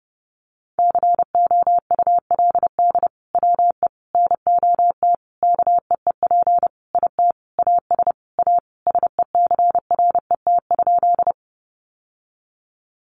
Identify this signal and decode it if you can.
Morse code, heard as a keyed tone, about 28 wpm, 700 Hz. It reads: COULD WE NOT KEEP IT AS A SECRET?
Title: Sentences from Top 1200 Words 30wpm